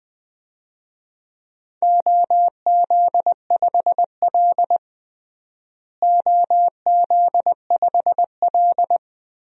Morse code, heard as a keyed tone, about 20 wpm, 700 Hz. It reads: OZ5L OZ5L
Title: DXCC-100 Call Signs - 4 Characters - Repeated 2x - 20wpm